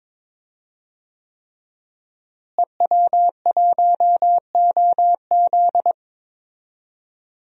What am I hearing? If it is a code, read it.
Morse code, heard as a keyed tone, about 21 wpm, 700 Hz. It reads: EW1OZ
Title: DXCC-100 Call Signs - 5 Characters - Repeated 3x - 22wpm